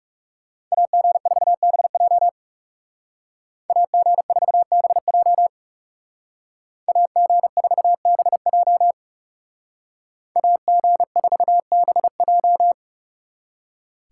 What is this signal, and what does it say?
Morse code, heard as a keyed tone, about 37 wpm, 700 Hz. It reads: AG4BJ AG4BJ AG4BJ AG4BJ
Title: Speed Racing - USA 2x2 Call Signs 30wpm